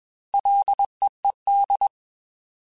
Morse code, heard as a keyed tone, about 21 wpm, 800 Hz. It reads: LEED